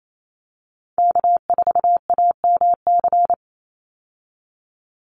Morse code, heard as a keyed tone, about 28 wpm, 700 Hz. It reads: K4AMC
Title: Call Signs Repeated 2x - 28wpm